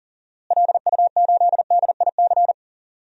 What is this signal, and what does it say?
Morse code, heard as a keyed tone, about 40 wpm, 700 Hz. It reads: LU8DIC
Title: DXCC-100 Call Signs - 6 Characters - Repeated 2x - 40wpm